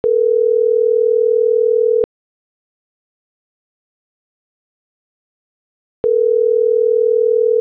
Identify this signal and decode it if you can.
Ringback tone (North America), 440 + 480 Hz.
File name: CENTRAL GENERAL / out ricardo.diaz - .22683